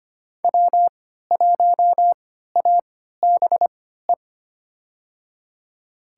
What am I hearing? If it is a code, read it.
Morse code, heard as a keyed tone, about 25 wpm, spaced as for 14 wpm, 700 Hz. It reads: W1ABE
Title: USA Call Signs - 5 Characters - 3x Character Spacing - 25wpm